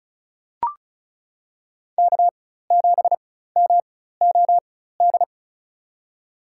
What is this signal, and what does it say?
Morse code, heard as a keyed tone, about 34 wpm, spaced as for 16 wpm, 700 Hz. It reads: K7MOD
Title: USA Call Signs - 5 Characters - 4x Character Spacing - 35wpm